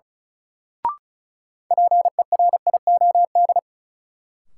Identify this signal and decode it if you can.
Morse code, heard as a keyed tone, about 36 wpm, 700 Hz. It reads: PERIOD
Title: Sets of 5 Words - New Word Drill - 35wpm